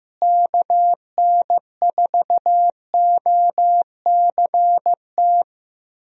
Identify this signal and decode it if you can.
Morse code, heard as a keyed tone, about 15 wpm, 700 Hz. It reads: KN4OCT
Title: USA Call Signs - 6 Characters - Repeated 3x - 15wpm